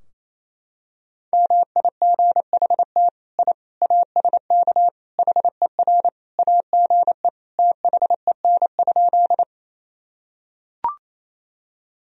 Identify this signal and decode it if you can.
Morse code, heard as a keyed tone, about 28 wpm, 700 Hz. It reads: MIGHT I ASK HER AGE THEN?
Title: Sentences from Top 1000 Words - Review Part 3 - 28wpm